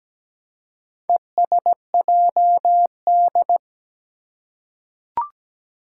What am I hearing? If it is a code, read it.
Morse code, heard as a keyed tone, about 17 wpm, 700 Hz. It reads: ESJD
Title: Four Challenging Letter-Number Combinations 17wpm